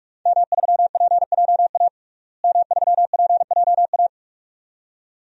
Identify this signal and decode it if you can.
Morse code, heard as a keyed tone, about 43 wpm, 700 Hz. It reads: M3PJA M3PJA
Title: DXCC-100 Call Signs - 5 Characters - Repeated 2x - 45wpm